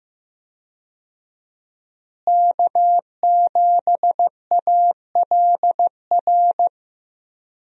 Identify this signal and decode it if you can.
Morse code, heard as a keyed tone, about 15 wpm, 700 Hz. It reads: K7ALR
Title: USA Call Signs - 5 Characters - Repeated 2x - 15wpm